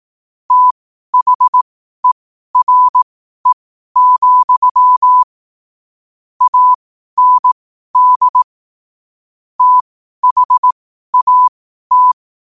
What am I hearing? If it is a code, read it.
Morse code, heard as a keyed tone, about 18 wpm, spaced as for 12 wpm, 1000 Hz. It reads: THERE, AND THAT